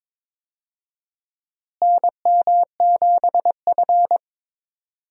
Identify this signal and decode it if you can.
Morse code, heard as a keyed tone, about 22 wpm, 700 Hz. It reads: NM7F
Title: USA Call Signs - 4 Characters - Repeated 3x - 22wpm